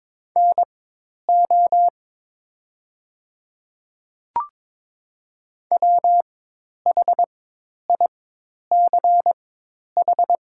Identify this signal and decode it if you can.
Morse code, heard as a keyed tone, about 22 wpm, spaced as for 10 wpm, 690 Hz. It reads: NO WHICH
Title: 4x Character Spacing - Top 100 Words - 22wpm